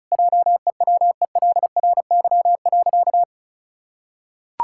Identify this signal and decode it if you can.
Morse code, heard as a keyed tone, about 36 wpm, 700 Hz. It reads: JEWELRY.
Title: Pangram Challenge v2 - 35wpm